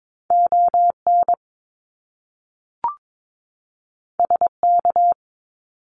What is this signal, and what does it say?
Morse code, heard as a keyed tone, about 22 wpm, 700 Hz. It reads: ON SK
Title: Canadian Provinces and Territories 22wpm